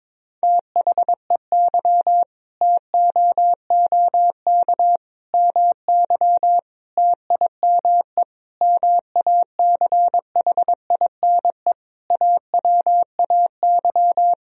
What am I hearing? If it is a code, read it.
Morse code, heard as a keyed tone, about 22 wpm, 700 Hz. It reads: THEY TOOK MY TIME MACHINE AWAY
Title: Sentences from Top 1000 Words - Review Part 2 - 22wpm